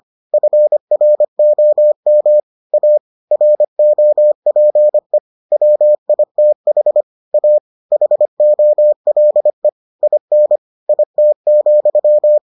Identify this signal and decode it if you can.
Morse code, heard as a keyed tone, about 25 wpm, 600 Hz. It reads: FROM A ROPE WITH A HOLE IN IT,